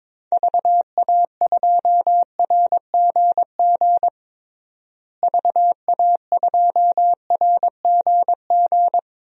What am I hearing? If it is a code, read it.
Morse code, heard as a keyed tone, about 22 wpm, 700 Hz. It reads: VA2RGG VA2RGG
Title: DXCC-100 Call Signs - 6 Characters - Repeated 2x - 22wpm